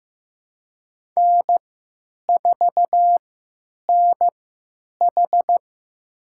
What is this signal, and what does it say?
Morse code, heard as a keyed tone, about 15 wpm, spaced as for 9 wpm, 700 Hz. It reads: N4NH
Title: USA Call Signs - 4 Characters - 3x Character Spacing - 15wpm